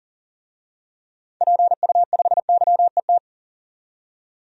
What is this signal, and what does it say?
Morse code, heard as a keyed tone, about 39 wpm, 700 Hz. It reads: PU5YET
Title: DXCC-100 Call Signs - 6 Characters - 40wpm